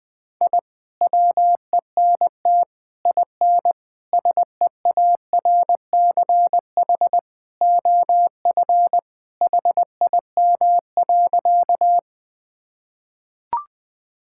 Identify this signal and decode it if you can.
Morse code, heard as a keyed tone, about 20 wpm, 700 Hz. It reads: I WENT IN SEARCH OF HIM.
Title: Sentences from Top 1000 Words - Review Part 5 - 20wpm